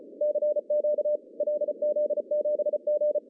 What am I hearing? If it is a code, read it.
Morse code, heard as a keyed tone, about 34 wpm, 600 Hz. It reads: CQ LZ7G
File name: CW / cqww-cw-k5tr